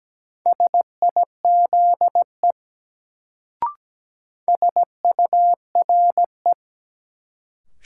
Morse code, heard as a keyed tone, about 17 wpm, 700 Hz. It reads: SIZE SURE
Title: Sets of 3 Words - New Word Drill - 17wpm